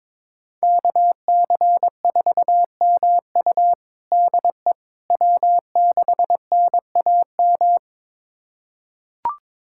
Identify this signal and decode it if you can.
Morse code, heard as a keyed tone, about 22 wpm, 700 Hz. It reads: KC4MU DE W6NAM
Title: USA Callsign DE Callsign - 5 Characters - 22wpm